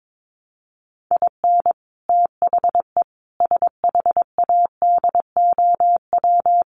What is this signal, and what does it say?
Morse code, heard as a keyed tone, about 22 wpm, 700 Hz. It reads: IN THE SHADOW
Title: Sets of 5 Words Encore 22wpm